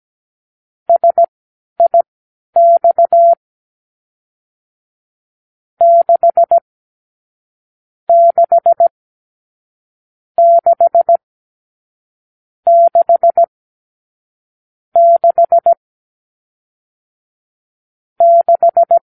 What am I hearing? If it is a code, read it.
Morse code, heard as a keyed tone, about 17 wpm, spaced as for 11 wpm, 680 Hz. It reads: SIX 6 6 6 6 6 6